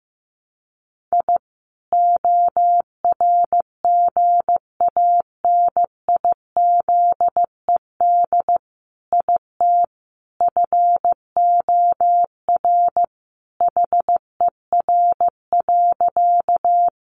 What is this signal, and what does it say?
Morse code, heard as a keyed tone, about 15 wpm, 700 Hz. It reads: I ORGANIZED IT FOR HER.
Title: Sentences from Top 1300 Words 15wpm